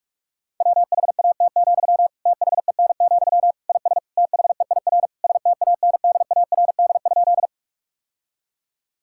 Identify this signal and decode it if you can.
Morse code, heard as a keyed tone, about 46 wpm, 700 Hz. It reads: WHAT, THEN, IS THEIR STANDARD?
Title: Sentences from Top 1000 Words - Review Part 6 - 45wpm